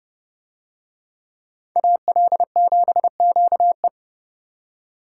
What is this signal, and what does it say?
Morse code, heard as a keyed tone, about 29 wpm, 700 Hz. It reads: AL7QE